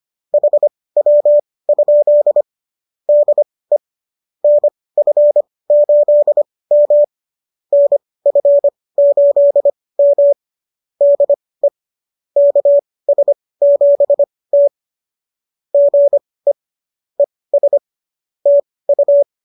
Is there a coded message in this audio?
Yes: Morse code from 0.3 to 19.2 s, heard as a keyed tone, about 25 wpm, spaced as for 18 wpm, 590 Hz. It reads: HW? DE NF8M NF8M DE KS7T GE ES TU